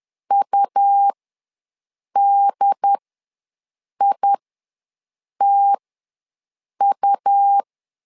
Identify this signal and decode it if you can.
Morse code, heard as a keyed tone, about 10 wpm, spaced as for 6 wpm, 800 Hz. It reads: UDITU
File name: LEZIONI / 2 UHDN / 2 UHD ws 10wpm 2058